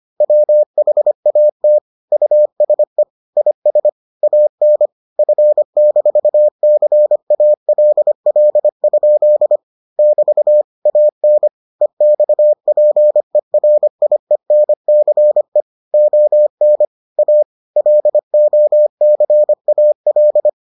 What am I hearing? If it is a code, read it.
Morse code, heard as a keyed tone, about 25 wpm, 600 Hz. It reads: WHAT USE IS AN F-CALL? <BT> AN EXPERIENCE ON A LOCAL